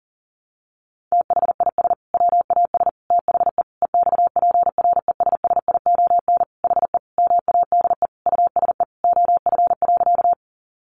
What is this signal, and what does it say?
Morse code, heard as a keyed tone, about 37 wpm, 700 Hz. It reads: THIS WAS THE EXPRESSION HE MADE USE OF.
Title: Sentences from Top 1200 Words 40wpm